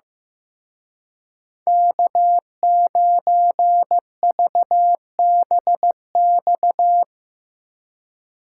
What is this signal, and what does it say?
Morse code, heard as a keyed tone, about 16 wpm, 700 Hz. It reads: K9VBX
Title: Speed Racing - USA 1x3 Call Signs 15wpm